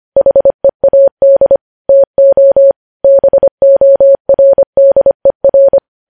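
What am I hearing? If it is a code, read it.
Morse code, heard as a keyed tone, about 25 wpm, 550 Hz. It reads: HEAD TO BORDER
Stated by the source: Morse news headlines sent at 25 wpm.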